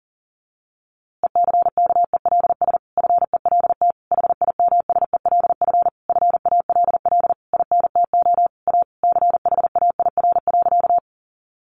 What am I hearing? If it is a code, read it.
Morse code, heard as a keyed tone, about 40 wpm, 700 Hz. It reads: ECKELS FELT HIMSELF FALL INTO A CHAIR.
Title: A Sound Of Thunder 40wpm